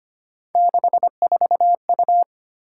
Morse code, heard as a keyed tone, about 25 wpm, 700 Hz. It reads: 64U
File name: Three Letter-Number Combinations - 25wpm